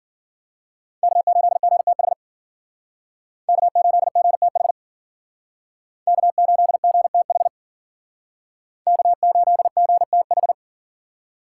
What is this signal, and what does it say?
Morse code, heard as a keyed tone, about 51 wpm, 700 Hz. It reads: K8GTH K8GTH K8GTH K8GTH